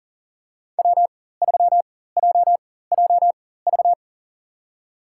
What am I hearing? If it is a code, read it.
Morse code, heard as a keyed tone, about 40 wpm, spaced as for 19 wpm, 700 Hz. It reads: W3JJV
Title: USA Call Signs - 5 Characters - 4x Character Spacing - 40wpm